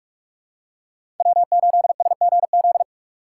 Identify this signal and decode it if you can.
Morse code, heard as a keyed tone, about 46 wpm, 700 Hz. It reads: W8SGZ
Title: USA Call Signs - 4x Character Spacing - 45wpm